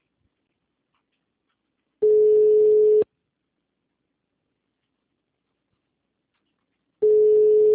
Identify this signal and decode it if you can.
Ringback tone (Europe, 425 Hz).
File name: out- -2049- - - .21759